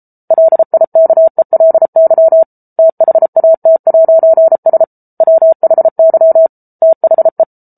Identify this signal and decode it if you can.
Morse code, heard as a keyed tone, about 34 wpm, 650 Hz. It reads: LIKELY THAT'S WHY THE